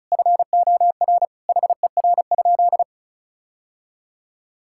Morse code, heard as a keyed tone, about 36 wpm, 700 Hz. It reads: FOR HER?
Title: Sentences from Top 1600 Words 35wpm